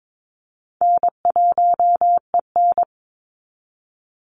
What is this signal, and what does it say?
Morse code, heard as a keyed tone, about 22 wpm, 700 Hz. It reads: N1EN
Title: Easy Call Signs Repeated 2x - 22wpm